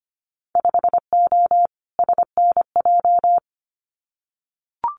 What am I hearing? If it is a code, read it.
Morse code, heard as a keyed tone, about 25 wpm, 690 Hz. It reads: 5O SNJ
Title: Winter Field Day Exchanges Encore 25wpm